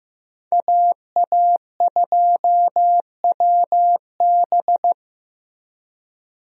Morse code, heard as a keyed tone, about 15 wpm, 700 Hz. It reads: AA2WB
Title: USA Call Signs - 5 Characters - Repeated 3x - 15wpm